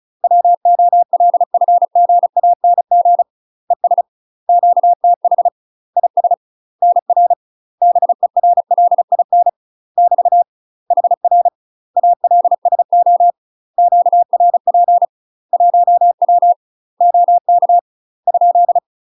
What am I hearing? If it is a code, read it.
Morse code, heard as a keyed tone, about 36 wpm, 700 Hz. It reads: WOLFGANG ES QTH IS NR BERLIN <BT> HR ALSO QRP 1W OK ?
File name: UK MASTER QSOs TRAINING / 35WPM / UK MASTER QSOs 0038